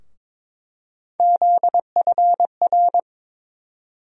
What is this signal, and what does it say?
Morse code, heard as a keyed tone, about 22 wpm, 700 Hz. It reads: ZFR
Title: USA Call Sign Postfixes - 3 Letters 22wpm